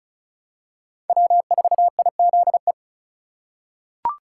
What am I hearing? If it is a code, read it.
Morse code, heard as a keyed tone, about 36 wpm, 700 Hz. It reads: W4IZE
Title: USA Call Signs - 5 Characters - 4x Character Spacing - 35wpm